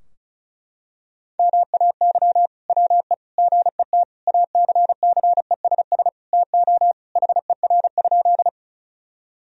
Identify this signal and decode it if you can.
Morse code, heard as a keyed tone, about 34 wpm, 700 Hz. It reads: MAY WE GET ACCESS TO HER?
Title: Sentences from Top 1000 Words - Review Part 4 - 35wpm